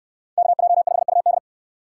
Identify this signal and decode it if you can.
Morse code, heard as a keyed tone, about 69 wpm, 700 Hz. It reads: KJ5UD